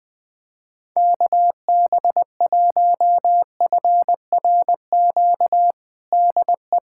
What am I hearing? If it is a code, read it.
Morse code, heard as a keyed tone, about 20 wpm, 700 Hz. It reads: KB1FRQ DE
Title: USA Callsign DE Callsign - 6 Characters - 20wpm